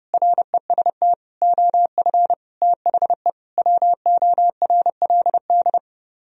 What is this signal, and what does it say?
Morse code, heard as a keyed tone, about 32 wpm, 700 Hz. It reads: REST OF THE WORLD